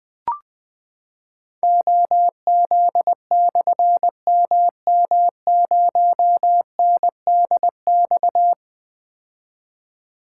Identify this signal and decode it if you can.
Morse code, heard as a keyed tone, about 20 wpm, 700 Hz. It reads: OZ/MM0NDX
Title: Difficult Call Signs 20wpm